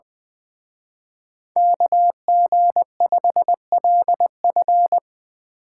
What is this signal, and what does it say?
Morse code, heard as a keyed tone, about 21 wpm, 700 Hz. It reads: KG5LF
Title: Speed Racing - USA 2x2 Call Signs 20wpm